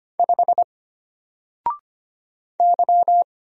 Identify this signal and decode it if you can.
Morse code, heard as a keyed tone, about 25 wpm, 700 Hz. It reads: 5 Y
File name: Single-Letter-Number-25wpm